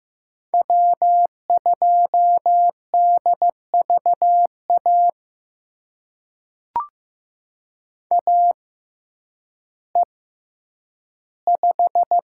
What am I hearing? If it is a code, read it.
Morse code, heard as a keyed tone, about 15 wpm, spaced as for 9 wpm, 700 Hz. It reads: W2DVA AE5
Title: USA Call Signs - 5 Characters - 6x Character Spacing - 15wpm